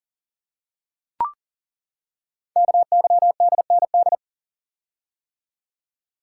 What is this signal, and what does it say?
Morse code, heard as a keyed tone, about 42 wpm, 700 Hz. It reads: KYDND
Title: Code Groups - 5 Letters - 40wpm